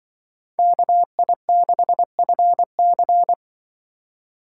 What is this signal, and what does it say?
Morse code, heard as a keyed tone, about 24 wpm, 700 Hz. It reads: KI6FC
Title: Speed Racing - USA 2x2 Call Signs 20wpm